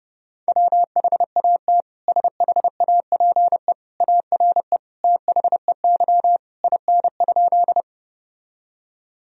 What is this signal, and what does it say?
Morse code, heard as a keyed tone, about 32 wpm, 700 Hz. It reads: WHAT SHAPE ARE THEY IN?
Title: Sentences from Top 1000 Words - Review Part 3 - 30wpm